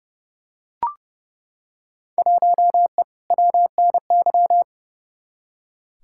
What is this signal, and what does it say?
Morse code, heard as a keyed tone, about 31 wpm, 700 Hz. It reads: 1E WNY
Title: Field Day Exchanges - 30wpm